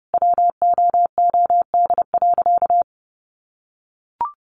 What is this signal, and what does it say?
Morse code, heard as a keyed tone, about 30 wpm, 700 Hz. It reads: WOOD.
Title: Sentences from Top 1100 Words 30wpm